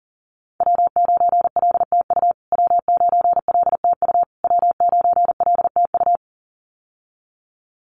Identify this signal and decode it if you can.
Morse code, heard as a keyed tone, about 40 wpm, 700 Hz. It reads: W9LTU W9LTU W9LTU
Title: USA Call Signs Repeated 3x 40wpm